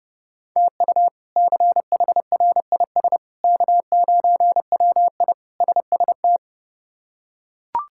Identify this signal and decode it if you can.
Morse code, heard as a keyed tone, about 30 wpm, 700 Hz. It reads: TU CHRIS K9WI SST
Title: K1USN SST QSO Practice - 30wpm